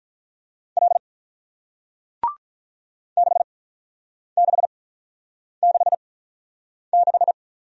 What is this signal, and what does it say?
Morse code, heard as a keyed tone, about 48 wpm, spaced as for 9 wpm, 700 Hz. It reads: L 6666